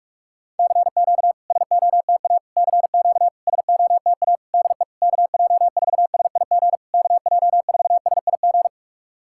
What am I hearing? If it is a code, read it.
Morse code, heard as a keyed tone, about 46 wpm, 690 Hz. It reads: CQ SOTA CQ SOTA DE KJ4SIG KJ4SIG
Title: SOTA QSO Practice 45wpm